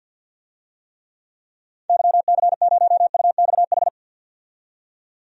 Morse code, heard as a keyed tone, about 52 wpm, 700 Hz. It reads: YC0UXH
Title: DXCC-100 Call Signs - 6 Characters - 50wpm